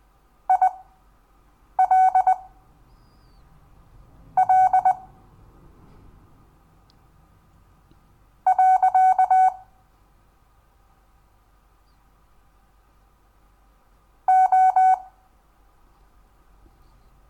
Morse code, heard as a keyed tone, about 19 wpm, spaced as for 4 wpm, 750 Hz. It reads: ILL. O
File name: cw / Lesson10 REVIEW